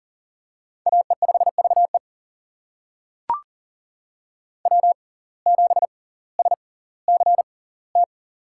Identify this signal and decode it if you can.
Morse code, heard as a keyed tone, about 39 wpm, spaced as for 20 wpm, 690 Hz. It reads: AE5VE W7SCT
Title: USA Call Signs - 5 Characters - 6x Character Spacing - 40wpm